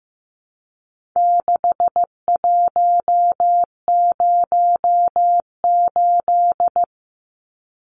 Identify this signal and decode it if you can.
Morse code, heard as a keyed tone, about 15 wpm, 700 Hz. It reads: 6108
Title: Numbers - 4-Digit 15wpm